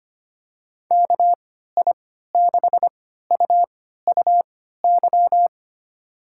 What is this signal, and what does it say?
Morse code, heard as a keyed tone, about 25 wpm, spaced as for 14 wpm, 700 Hz. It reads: KI6UUY
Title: USA Call Signs - 6 Characters - 3x Character Spacing - 25wpm